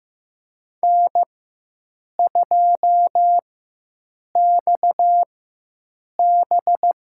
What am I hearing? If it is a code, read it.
Morse code, heard as a keyed tone, about 15 wpm, spaced as for 7 wpm, 700 Hz. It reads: N2XB
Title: USA Call Signs - 4 Characters - 4x Character Spacing - 15wpm